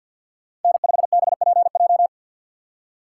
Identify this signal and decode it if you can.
Morse code, heard as a keyed tone, about 46 wpm, 700 Hz. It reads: N5BPJ
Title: USA Call Signs - 5 Characters - 4x Character Spacing - 50wpm